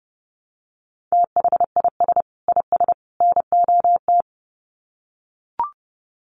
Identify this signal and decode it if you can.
Morse code, heard as a keyed tone, about 30 wpm, 700 Hz. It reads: THIS IS NOT
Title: Sets of 3 Words 30wpm